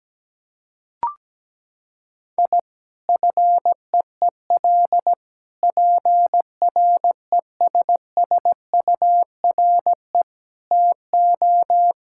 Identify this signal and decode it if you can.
Morse code, heard as a keyed tone, about 17 wpm, 700 Hz. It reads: I FEEL PRESSURE TO
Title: Sentences from Top 1000 Words - Review Part 6 - 17wpm